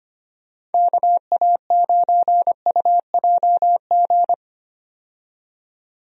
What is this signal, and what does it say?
Morse code, heard as a keyed tone, about 25 wpm, 700 Hz. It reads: KA9UJG
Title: USA Call Signs - 6 Characters - 25wpm